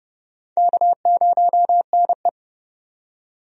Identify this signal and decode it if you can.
Morse code, heard as a keyed tone, about 32 wpm, 700 Hz. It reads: K0NE
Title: USA Call Signs - 4 Characters - 30wpm